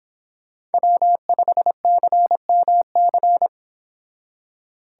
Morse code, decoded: W5CMC

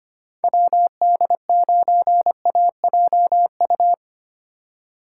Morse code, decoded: WD9AJU

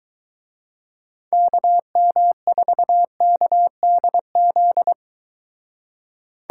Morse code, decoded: KM4KDZ

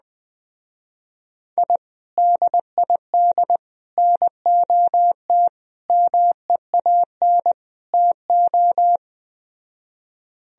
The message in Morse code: I DID NOT MEAN TO

